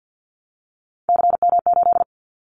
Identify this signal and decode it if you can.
Morse code, heard as a keyed tone, about 48 wpm, 700 Hz. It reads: CM8
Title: Callsign Prefixes for Top 100 DXCC - 50wpm